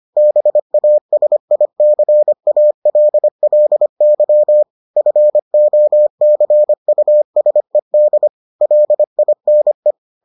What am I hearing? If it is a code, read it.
Morse code, heard as a keyed tone, about 25 wpm, 600 Hz. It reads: BASICALLY FOCUSED LINE